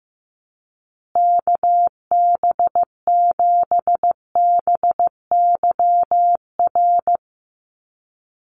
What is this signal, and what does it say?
Morse code, heard as a keyed tone, about 15 wpm, 700 Hz. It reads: KB7BYR
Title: USA Call Signs Repeated 2x 15wpm